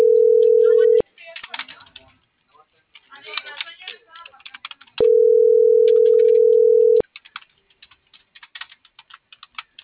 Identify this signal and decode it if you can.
Ringback tone (North America), 440 + 480 Hz.
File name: CENTRAL MABE / out sebastian.espinosa